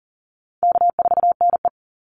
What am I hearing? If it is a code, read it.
Morse code, heard as a keyed tone, about 40 wpm, 700 Hz. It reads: K4NE
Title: Easy Call Signs Repeated 3x - 40wpm